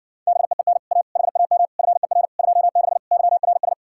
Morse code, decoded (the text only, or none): BEEN A HAM FER 26 YRS